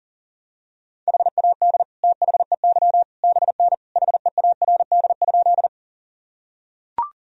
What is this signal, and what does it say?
Morse code, heard as a keyed tone, about 40 wpm, 700 Hz. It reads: HAD THEY BN HEARD?
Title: Sentences from Top 400 Words and Abbreviations 40wpm